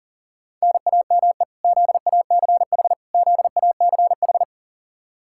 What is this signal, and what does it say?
Morse code, heard as a keyed tone, about 40 wpm, 700 Hz. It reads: NAME ZACH ZACH
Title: Practice QSO - Exchange 1 - Beginning - 40wpm